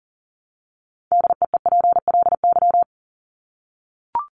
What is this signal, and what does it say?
Morse code, heard as a keyed tone, about 40 wpm, 700 Hz. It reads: DEEPLY
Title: Top 1751-2000 Words 40wpm